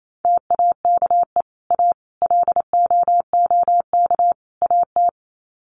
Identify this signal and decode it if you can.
Morse code, heard as a keyed tone, about 28 wpm, 700 Hz. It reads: TAKE A LOOK AT
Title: Sets of 4 Words 28wpm